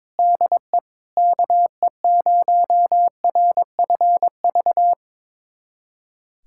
Morse code, heard as a keyed tone, about 21 wpm, 700 Hz. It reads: DE KE0RFV
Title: USA Callsign DE Callsign - 6 Characters - 22wpm